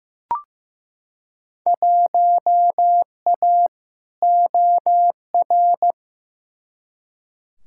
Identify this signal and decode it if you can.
Morse code, heard as a keyed tone, about 15 wpm, 700 Hz. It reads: 1A OR
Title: Field Day Exchanges - 15wpm